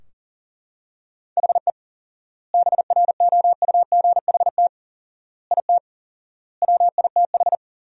Morse code, heard as a keyed tone, about 40 wpm, 700 Hz. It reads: HE BROUGHT IT WITH
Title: 4x Word-Spacing - Sentences from Top 500 Words - 40wpm